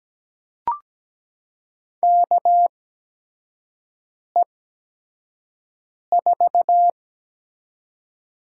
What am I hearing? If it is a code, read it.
Morse code, heard as a keyed tone, about 17 wpm, spaced as for 5 wpm, 700 Hz. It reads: KE4
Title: USA Call Signs - 6 Characters - 8x Character Spacing - 17wpm